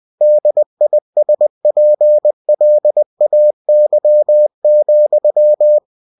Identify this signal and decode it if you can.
Morse code, heard as a keyed tone, about 20 wpm, 600 Hz. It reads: DISPLAY,